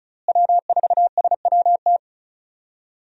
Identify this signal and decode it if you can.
Morse code, heard as a keyed tone, about 34 wpm, 700 Hz. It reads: W4SWT
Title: USA Call Signs - 5 Characters - Repeated 2x - 35wpm